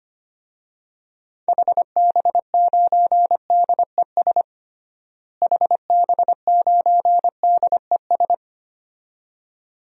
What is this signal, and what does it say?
Morse code, heard as a keyed tone, about 25 wpm, 700 Hz. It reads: HB9DES HB9DES
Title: DXCC-100 Call Signs - 6 Characters - Repeated 2x - 25wpm